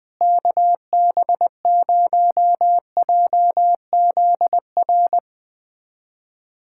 Morse code, heard as a keyed tone, about 20 wpm, 700 Hz. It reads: KB0JZR
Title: USA Callsign DE Callsign - 6 Characters - 20wpm